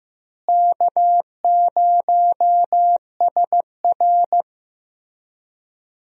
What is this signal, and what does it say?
Morse code, heard as a keyed tone, about 15 wpm, 700 Hz. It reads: K0SR
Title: USA Call Signs - 4 Characters - Repeated 2x - 15wpm